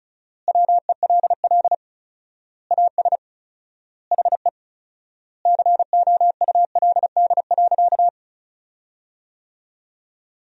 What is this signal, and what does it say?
Morse code, heard as a keyed tone, about 34 wpm, 700 Hz. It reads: WELL AS HE COULD.
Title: 4x Word-Spacing - Sentences from Top 500 Words - 35wpm